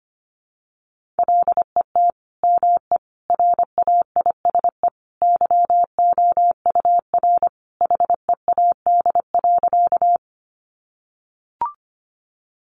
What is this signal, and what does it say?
Morse code, heard as a keyed tone, about 25 wpm, 700 Hz. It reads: LET ME RAISE YOUR HEAD.
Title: Sentences from Top 1400 Words 25wpm